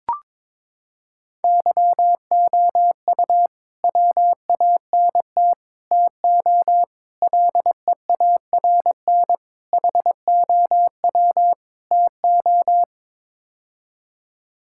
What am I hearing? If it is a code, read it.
Morse code, heard as a keyed tone, about 22 wpm, 700 Hz. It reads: YOU WANT TO LEARN HOW TO
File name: practice / 22wpm / Sets of 6 Words - 22wpm